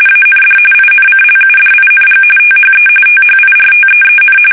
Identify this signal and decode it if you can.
RTTY, about 45.45 baud. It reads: AND INCLUDING THE THIRTEEN
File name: fsk850 / usnavy1